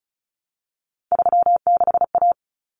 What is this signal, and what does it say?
Morse code, heard as a keyed tone, about 34 wpm, 700 Hz. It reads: 36A
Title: Common Cut Numbers - 3-Digit 35wpm